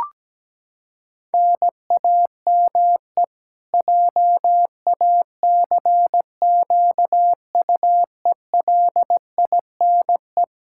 Morse code, heard as a keyed tone, about 17 wpm, 700 Hz. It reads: NAME JACQUELINE
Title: K1USN SST QSO Practice 17wpm